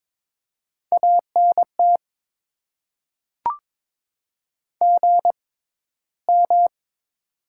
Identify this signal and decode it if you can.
Morse code, heard as a keyed tone, about 21 wpm, spaced as for 13 wpm, 700 Hz. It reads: ANT GM
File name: Top 20 QSO Elements - 6x Character Spacing - 22wpm